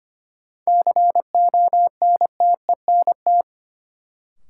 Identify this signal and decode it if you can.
Morse code, heard as a keyed tone, about 25 wpm, 700 Hz. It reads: CONTENT
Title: Sentences from Top 900 Words - New Word Drill - 25wpm